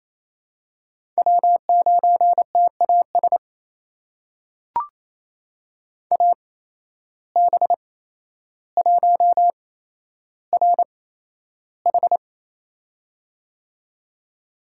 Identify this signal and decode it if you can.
Morse code, heard as a keyed tone, about 28 wpm, spaced as for 12 wpm, 700 Hz. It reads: W9TAS AB1RH
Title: USA Call Signs - 5 Characters - 8x Character Spacing - 28wpm